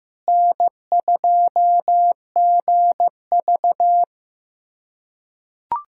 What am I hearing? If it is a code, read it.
Morse code, heard as a keyed tone, about 16 wpm, 700 Hz. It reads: N2GV
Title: USA Callsign DE Callsign - 4 Characters - 15wpm